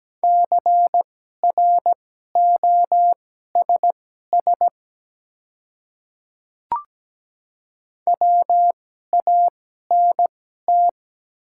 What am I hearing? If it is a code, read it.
Morse code, heard as a keyed tone, about 17 wpm, spaced as for 12 wpm, 700 Hz. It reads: CROSS WANT
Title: 2x Character Spacing - Top 300 Words - 17wpm